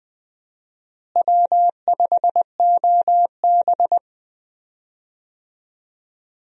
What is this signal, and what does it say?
Morse code, heard as a keyed tone, about 20 wpm, 700 Hz. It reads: W5OB
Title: USA Call Signs - 4 Characters - Repeated 2x - 20wpm